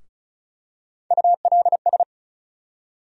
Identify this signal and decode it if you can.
Morse code, heard as a keyed tone, about 34 wpm, 700 Hz. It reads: ULS
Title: USA Call Sign Postfixes - 3 Letters 35wpm